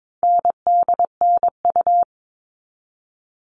Morse code, heard as a keyed tone, about 22 wpm, 700 Hz. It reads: NDNU
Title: All Cut Numbers - 4-Digit 22wpm